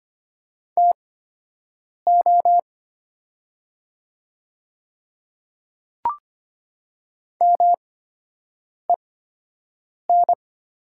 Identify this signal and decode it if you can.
Morse code, heard as a keyed tone, about 25 wpm, spaced as for 7 wpm, 700 Hz. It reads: TO MEN
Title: 8x Character Spacing - Top 200 Words - 25wpm